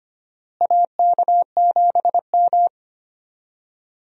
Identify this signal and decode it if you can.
Morse code, heard as a keyed tone, about 25 wpm, 700 Hz. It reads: AK7M
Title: USA Call Signs - 4 Characters - Repeated 2x - 25wpm